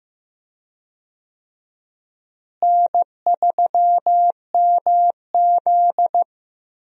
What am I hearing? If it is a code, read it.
Morse code, heard as a keyed tone, about 15 wpm, 700 Hz. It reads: N3MZ
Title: USA Call Signs - 4 Characters - Repeated 2x - 15wpm